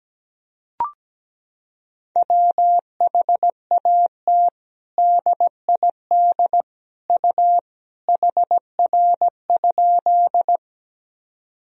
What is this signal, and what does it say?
Morse code, heard as a keyed tone, about 17 wpm, 700 Hz. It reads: WHAT DID U HR?
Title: Sentences from Top 300 Words and Abbreviations 17wpm